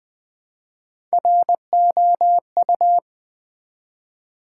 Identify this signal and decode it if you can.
Morse code, heard as a keyed tone, about 20 wpm, 700 Hz. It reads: ROU
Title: Build A Word - 1 Letter At A Time - 3 to 5-Letter Words - 20wpm